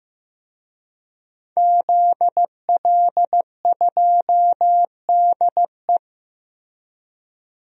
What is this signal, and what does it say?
Morse code, heard as a keyed tone, about 15 wpm, 700 Hz. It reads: ZL2DE
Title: DXCC-100 Call Signs - 5 Characters - Repeated 3x - 15wpm